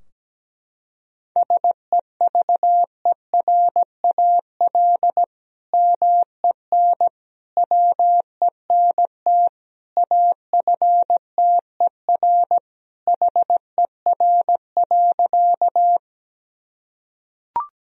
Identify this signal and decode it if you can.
Morse code, heard as a keyed tone, about 17 wpm, 700 Hz. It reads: SEVERAL MEN WENT AFTER HER.
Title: Sentences from Top 1000 Words - Review Part 2 - 17wpm